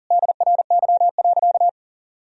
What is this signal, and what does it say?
Morse code, heard as a keyed tone, about 40 wpm, 700 Hz. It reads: DRY.
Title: Sentences from Top 1000 Words - Review Part 4 - 40wpm